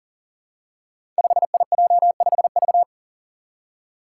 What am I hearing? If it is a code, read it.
Morse code, heard as a keyed tone, about 40 wpm, 700 Hz. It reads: 5IJ5V
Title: Five Challenging Letter-Number Combinations 40wpm